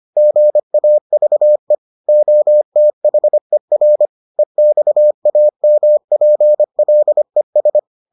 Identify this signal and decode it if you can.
Morse code, heard as a keyed tone, about 25 wpm, 600 Hz. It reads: GAVE OTHER EXAMPLES